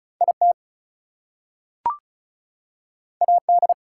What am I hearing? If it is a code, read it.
Morse code, heard as a keyed tone, about 36 wpm, 700 Hz. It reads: IT AD
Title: Two Letter Words - 35wpm